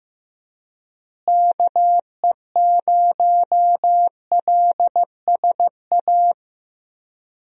Morse code, heard as a keyed tone, about 15 wpm, 700 Hz. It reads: KE0LSA
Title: USA Call Signs - 6 Characters - Repeated 2x - 15wpm